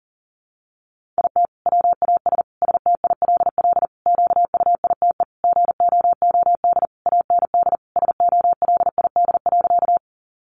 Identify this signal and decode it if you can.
Morse code, heard as a keyed tone, about 40 wpm, 700 Hz. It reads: IT WAS STILL QUITE GOOD AND SOLID.
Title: Sentences from Top 1500 Words 40wpm